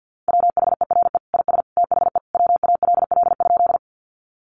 Morse code, heard as a keyed tone, about 49 wpm, 700 Hz. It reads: WHERE IS THE WALL?
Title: Sentences from Top 1100 Words 50wpm